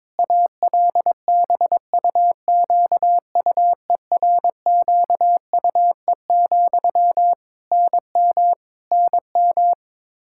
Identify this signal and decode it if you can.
Morse code, heard as a keyed tone, about 22 wpm, 700 Hz. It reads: ALBUQUERQUE, NM NM
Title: Practice QSO - Exchange 1 - Beginning - 22wpm